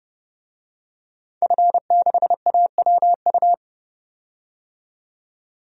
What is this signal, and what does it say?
Morse code, heard as a keyed tone, about 31 wpm, 700 Hz. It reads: F6AWU
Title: DXCC-100 Call Signs - 5 Characters - 30wpm